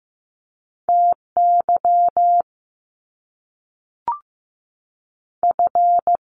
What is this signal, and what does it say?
Morse code, heard as a keyed tone, about 15 wpm, 700 Hz. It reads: T Y F
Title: Top Two Letter Combinations in Words 15wpm